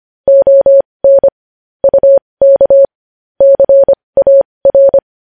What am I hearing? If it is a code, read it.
Morse code, heard as a keyed tone, about 25 wpm, spaced as for 20 wpm, 550 Hz. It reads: ON UK CAR
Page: news headlines in morse code at 20 wpm